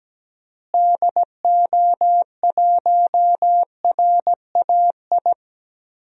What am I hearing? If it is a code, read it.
Morse code, heard as a keyed tone, about 17 wpm, 700 Hz. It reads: DO1RAI
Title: DXCC-100 Call Signs - 6 Characters - Repeated 2x - 17wpm